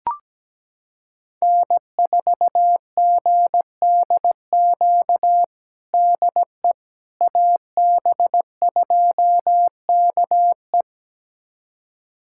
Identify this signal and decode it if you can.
Morse code, heard as a keyed tone, about 17 wpm, 700 Hz. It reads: N4GDQ DE AB2KE
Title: USA Callsign DE Callsign - 5 Characters - 17wpm